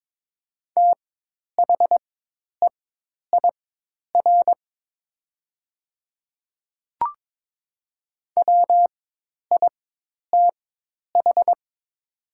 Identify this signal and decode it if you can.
Morse code, heard as a keyed tone, about 22 wpm, spaced as for 10 wpm, 700 Hz. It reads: THEIR WITH
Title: 4x Character Spacing - Top 100 Words - 22wpm